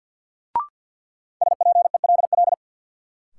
Morse code, heard as a keyed tone, about 47 wpm, 700 Hz. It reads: SPELL
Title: Sentences from Top 1000 Words - Review Part 1 - New Word Drill - 50wpm